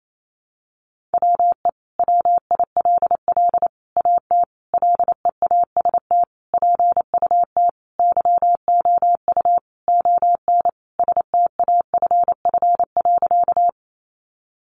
Morse code, heard as a keyed tone, about 28 wpm, 700 Hz. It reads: WE WILL AT LEAST PUT YOU ON STAFF.
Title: Sentences from Top 1200 Words 28wpm